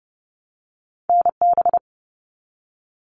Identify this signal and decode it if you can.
Morse code, heard as a keyed tone, about 30 wpm, 700 Hz. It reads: NB
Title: Canadian Provinces and Territories 30wpm